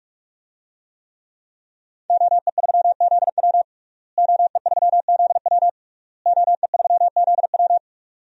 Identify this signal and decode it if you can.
Morse code, heard as a keyed tone, about 45 wpm, 700 Hz. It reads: OE3ZW OE3ZW OE3ZW
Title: DXCC-100 Call Signs - 5 Characters - Repeated 3x - 45wpm